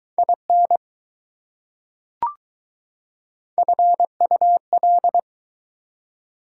Morse code, heard as a keyed tone, about 20 wpm, 700 Hz. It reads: IN FUL